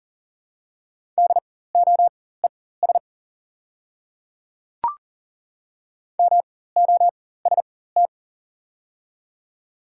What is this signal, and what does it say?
Morse code, heard as a keyed tone, about 40 wpm, spaced as for 19 wpm, 700 Hz. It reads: DOES MOST
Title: 4x Character Spacing - Top 300 Words - 40wpm